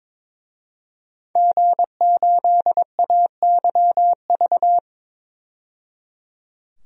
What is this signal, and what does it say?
Morse code, heard as a keyed tone, about 22 wpm, 700 Hz. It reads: G8AYV
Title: DXCC-100 Call Signs - 5 Characters - 22wpm